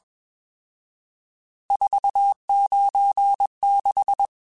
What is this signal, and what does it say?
Morse code, heard as a keyed tone, about 21 wpm, 800 Hz. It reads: 496